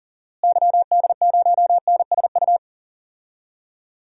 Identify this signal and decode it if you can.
Morse code, heard as a keyed tone, about 40 wpm, 700 Hz. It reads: YD0NSU
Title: DXCC-100 Call Signs - 6 Characters - Repeated 3x - 40wpm